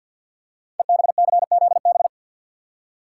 Morse code, heard as a keyed tone, about 47 wpm, 700 Hz. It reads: EBCZB
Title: Code Groups - 5 Letters - 50wpm